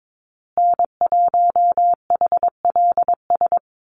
Morse code, heard as a keyed tone, about 22 wpm, 690 Hz. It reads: N1HLS
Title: USA Call Signs Repeated 3x 22wpm